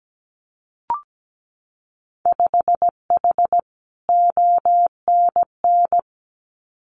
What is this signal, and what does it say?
Morse code, heard as a keyed tone, about 17 wpm, 700 Hz. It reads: 5H ONN
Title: Winter Field Day Exchanges Encore 17wpm